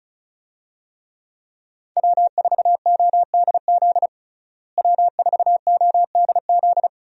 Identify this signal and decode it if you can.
Morse code, heard as a keyed tone, about 36 wpm, 700 Hz. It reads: W4ODZ W4ODZ
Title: USA Call Signs - 5 Characters - Repeated 3x - 35wpm